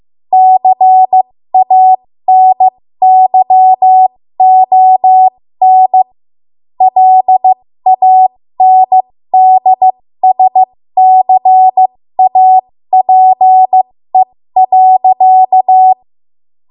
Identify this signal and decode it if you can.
Morse code, heard as a keyed tone, about 15 wpm, spaced as for 13 wpm, 750 Hz. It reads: CANYON LANDSCAPE.
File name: Archive / 13 WPM / 13WPM